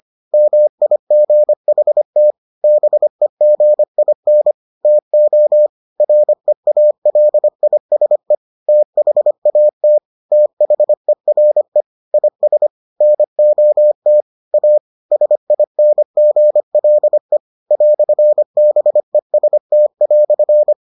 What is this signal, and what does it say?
Morse code, heard as a keyed tone, about 25 wpm, 600 Hz. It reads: MIGHT BEGIN TO REALISE THAT THERE IS NOT A SINGLE "BEST"